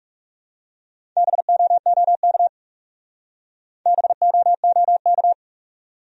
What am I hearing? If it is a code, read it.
Morse code, heard as a keyed tone, about 34 wpm, 700 Hz. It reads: BOOK BOOK
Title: Speed-Racing Top 300 Words 30wpm